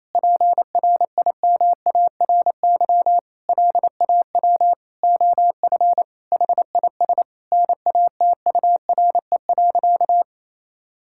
Morse code, heard as a keyed tone, about 28 wpm, 700 Hz. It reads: PRIMARY LAW OF HIS NATURE.
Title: Sentences from Top 1000 Words - Review Part 5 - 28wpm